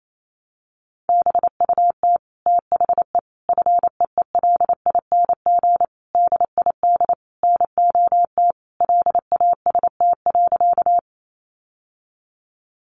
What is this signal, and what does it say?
Morse code, heard as a keyed tone, about 28 wpm, 700 Hz. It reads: BUT THE FEELING DID NOT LAST.
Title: Sentences from Top 1200 Words 28wpm